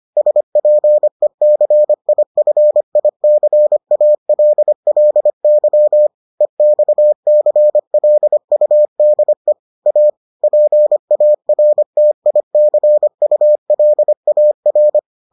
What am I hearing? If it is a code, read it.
Morse code, heard as a keyed tone, about 25 wpm, 600 Hz. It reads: SPECIFICALLY EXCLUDE A PARTICULAR